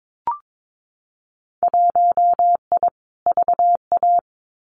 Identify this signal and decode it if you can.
Morse code, heard as a keyed tone, about 22 wpm, 700 Hz. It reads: 1I VA
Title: Winter Field Day Exchanges 22wpm